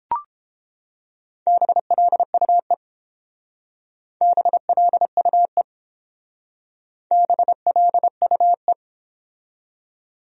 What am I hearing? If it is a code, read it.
Morse code, heard as a keyed tone, about 28 wpm, 700 Hz. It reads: BLUE BLUE BLUE